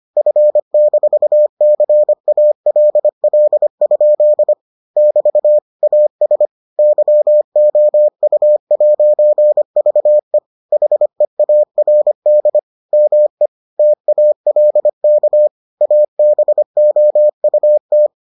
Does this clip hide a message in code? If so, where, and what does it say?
Yes: Morse code from 0.2 to 18.1 s, heard as a keyed tone, about 25 wpm, 600 Hz. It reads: F-CALL? <BT> AS YOU'VE HEARD ME TALK ABOUT